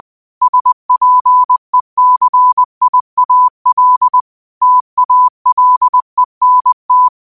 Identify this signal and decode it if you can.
Morse code, heard as a keyed tone, about 20 wpm, 1000 Hz. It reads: SPECIAL TALENT
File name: qotdfiles / 20wpm